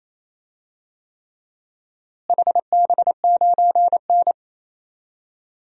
Morse code, heard as a keyed tone, about 27 wpm, 700 Hz. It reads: HB9N
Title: DXCC-100 Call Signs - 4 Characters - Repeated 3x - 28wpm